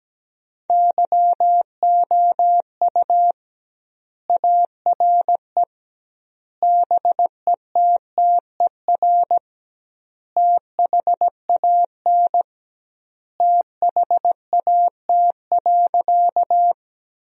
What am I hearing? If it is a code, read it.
Morse code, heard as a keyed tone, about 17 wpm, 700 Hz. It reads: YOU ARE BETTER THAN THAT.
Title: 2x Word-Spacing - Sentences from Top 400 Words - 17wpm